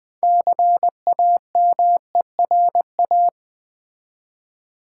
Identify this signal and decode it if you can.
Morse code, heard as a keyed tone, about 20 wpm, 700 Hz. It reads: CAMERA
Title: Easy Words with Hints - 20wpm